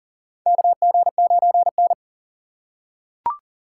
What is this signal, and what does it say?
Morse code, heard as a keyed tone, about 39 wpm, 700 Hz. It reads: KG9N
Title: USA Call Signs - 4 Characters - 3x Character Spacing - 40wpm